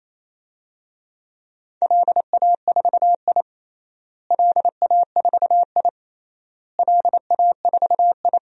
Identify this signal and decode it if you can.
Morse code, heard as a keyed tone, about 28 wpm, 700 Hz. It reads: LA4I LA4I LA4I
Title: DXCC-100 Call Signs - 4 Characters - Repeated 3x - 28wpm